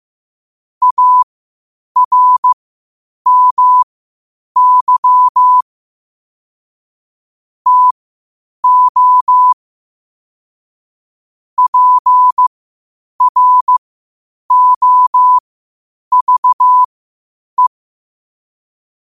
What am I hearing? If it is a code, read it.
Morse code, heard as a keyed tone, about 15 wpm, spaced as for 8 wpm, 1000 Hz. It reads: ARMY TO PROVE
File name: qotdfiles / 8wpm